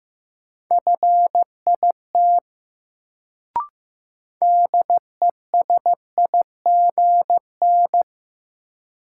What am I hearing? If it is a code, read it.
Morse code, heard as a keyed tone, about 15 wpm, 700 Hz. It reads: FIT DESIGN
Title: Sentences from Top 800 Words - New Word Drill - 15wpm